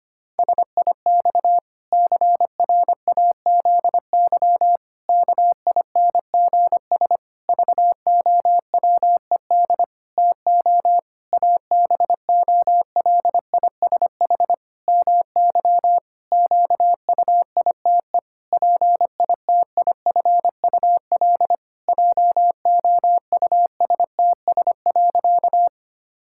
Morse code, heard as a keyed tone, about 25 wpm, 700 Hz. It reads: SIX CRAZY KINGS VOWED TO ABOLISH MY QUITE PITIFUL JOUSTS.
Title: Pangram Challenge v2 - 25wpm